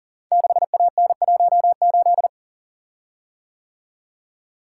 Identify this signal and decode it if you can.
Morse code, heard as a keyed tone, about 38 wpm, 700 Hz. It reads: 6AN18
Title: Code Groups - 5 Letter:Numbers - 40wpm